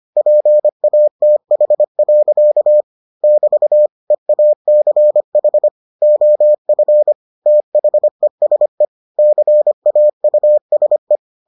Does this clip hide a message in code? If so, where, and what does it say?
Yes: Morse code from 0.2 to 11.2 s, heard as a keyed tone, about 25 wpm, 600 Hz. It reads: PATH. <BT> EACH OF THESE CAUSE